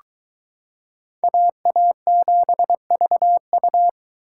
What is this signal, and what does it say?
Morse code, heard as a keyed tone, about 23 wpm, 700 Hz. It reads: AA7VU